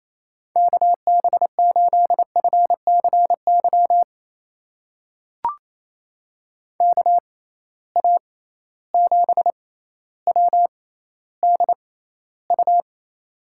Morse code, heard as a keyed tone, about 28 wpm, spaced as for 14 wpm, 700 Hz. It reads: KB8FCY KA7WDU
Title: USA Call Signs - 6 Characters - 6x Character Spacing - 28wpm